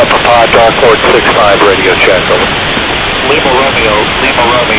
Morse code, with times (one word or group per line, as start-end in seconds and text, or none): none